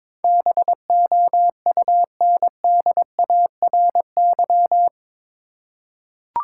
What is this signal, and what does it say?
Morse code, heard as a keyed tone, about 22 wpm, 700 Hz. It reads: BOUNDARY
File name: practice / 22wpm / Eight Letter Words - 22wpm